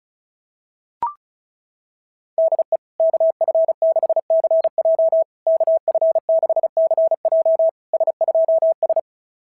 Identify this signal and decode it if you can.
Morse code, heard as a keyed tone, about 36 wpm, 690 Hz. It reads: DE KF6CJ KF6CJ S2S
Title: SOTA QSO Practice 35wpm